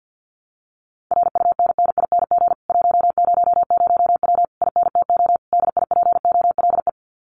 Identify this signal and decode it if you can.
Morse code, heard as a keyed tone, about 48 wpm, 700 Hz. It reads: RUNNING 100W INTO DIPOLE
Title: Common QSO Phrases Encore 50wpm